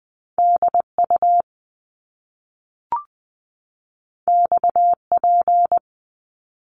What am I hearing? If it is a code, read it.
Morse code, heard as a keyed tone, about 20 wpm, 700 Hz. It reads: DU XP
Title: Top Two Letter Combinations in Words 20wpm